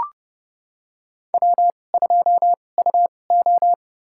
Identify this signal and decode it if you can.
Morse code, heard as a keyed tone, about 30 wpm, spaced as for 22 wpm, 700 Hz. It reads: W2UO
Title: USA Call Signs - 4 Characters - 2x Character Spacing - 30wpm